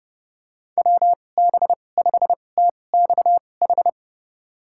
Morse code, heard as a keyed tone, about 30 wpm, spaced as for 22 wpm, 700 Hz. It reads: WB5TXH
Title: USA Call Signs - 6 Characters - 2x Character Spacing - 30wpm